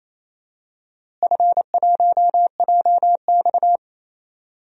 Morse code, heard as a keyed tone, about 28 wpm, 700 Hz. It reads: F1JX
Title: DXCC-100 Call Signs - 4 Characters - Repeated 3x - 28wpm